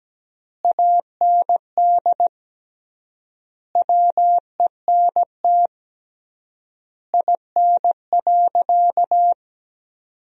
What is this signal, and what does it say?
Morse code, heard as a keyed tone, about 17 wpm, 700 Hz. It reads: AND WENT IN.
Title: 3x Word-Spacing - Sentences from Top 300 Words - 17wpm